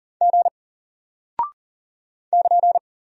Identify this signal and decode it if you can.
Morse code, heard as a keyed tone, about 41 wpm, 700 Hz. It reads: G <KN>
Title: Single Character (Letter:Number:Punctuation:Prosign) Alt 40wpm